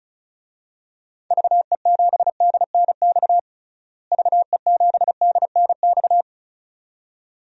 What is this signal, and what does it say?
Morse code, heard as a keyed tone, about 34 wpm, 700 Hz. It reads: VE7DNX VE7DNX
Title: DXCC-100 Call Signs - 6 Characters - Repeated 2x - 35wpm